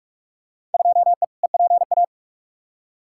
Morse code, heard as a keyed tone, about 46 wpm, 700 Hz. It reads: 2E EPA
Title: Field Day Exchanges Encore - 45wpm